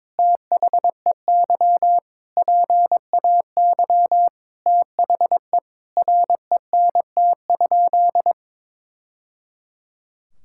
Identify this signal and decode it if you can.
Morse code, heard as a keyed tone, about 22 wpm, 700 Hz. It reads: THEY PAY THE RENT?
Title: Sentences from Top 1000 Words - Review Part 5 - 22wpm